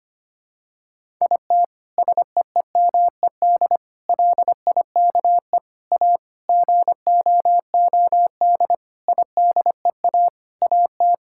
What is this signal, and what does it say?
Morse code, heard as a keyed tone, about 25 wpm, 700 Hz. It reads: IT SEEMED LIKE A GOOD IDEA AT
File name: Sets of 9 Words - 25wpm